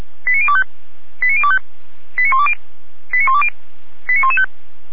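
Five-tone selective call (ZVEI), heard as quick runs of five tones, tones of about 70 ms. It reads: 89025 89025 89120 89120 89115
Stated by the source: ZVEI five-tone selective call received on an SDR.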